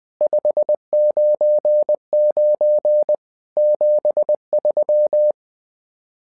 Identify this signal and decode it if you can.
Morse code, heard as a keyed tone, about 20 wpm, 600 Hz. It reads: 599 73
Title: SOTA QSO Practice 20wpm